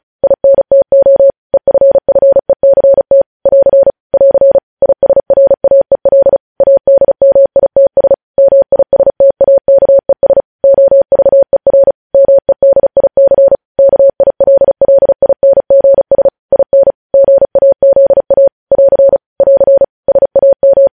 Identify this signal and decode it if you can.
Morse code, heard as a keyed tone, about 36 wpm, 550 Hz. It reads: INTO EFFECT <AR> <AR> ISRAEL ADMITS MISTAKES OVER MEDIC KILLINGS IN GAZA <AR> <AR> SAM
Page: News Headlines in Morse Code at 35 WPM